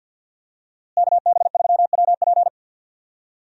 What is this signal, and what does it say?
Morse code, heard as a keyed tone, about 51 wpm, 700 Hz. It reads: KB3WP
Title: USA Call Signs - 5 Characters - Repeated 2x - 50wpm